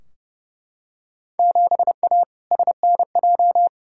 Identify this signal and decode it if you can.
Morse code, heard as a keyed tone, about 30 wpm, 700 Hz. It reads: 7A SNJ
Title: Field Day Exchanges Encore - 30wpm